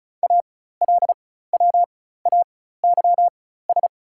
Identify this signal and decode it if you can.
Morse code, heard as a keyed tone, about 36 wpm, spaced as for 16 wpm, 700 Hz. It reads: ALWAYS
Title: 4x Character Spacing - Top 300 Words - 35wpm